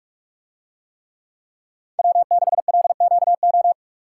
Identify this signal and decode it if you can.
Morse code, heard as a keyed tone, about 45 wpm, 700 Hz. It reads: W6LQO